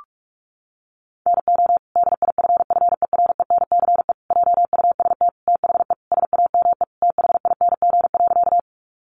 Morse code, heard as a keyed tone, about 44 wpm, 700 Hz. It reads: NO DIFFERENCE JUST THE SAME THING.
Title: Sentences from Top 800 Words 45wpm